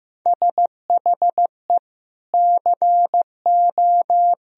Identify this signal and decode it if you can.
Morse code, heard as a keyed tone, about 15 wpm, 700 Hz. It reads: SHE CO
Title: Sentences from Top 1000 Words - Review Part 3 - 15wpm